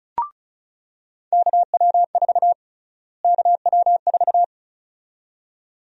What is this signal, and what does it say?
Morse code, heard as a keyed tone, about 36 wpm, 700 Hz. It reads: KW4 KW4
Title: USA Call Sign Prefixes - Repeated 2x - 35wpm